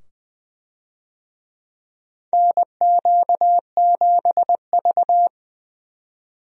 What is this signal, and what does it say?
Morse code, heard as a keyed tone, about 20 wpm, 700 Hz. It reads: NQ7V
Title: USA Call Signs - 4 Characters - 20wpm